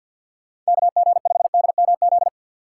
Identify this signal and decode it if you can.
Morse code, heard as a keyed tone, about 49 wpm, 700 Hz. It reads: KG5DMZ
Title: USA Call Signs - 6 Characters - 3x Character Spacing - 50wpm